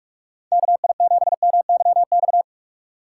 Morse code, heard as a keyed tone, about 46 wpm, 700 Hz. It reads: KI7MYX
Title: USA Call Signs - 6 Characters - 2x Character Spacing - 45wpm